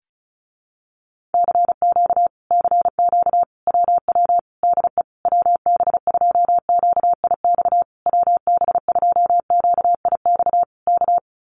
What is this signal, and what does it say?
Morse code, heard as a keyed tone, about 34 wpm, 700 Hz. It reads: CQ CQ WW DE WB2QIX WB2QIX K
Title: Common QSO Phrases Encore 35wpm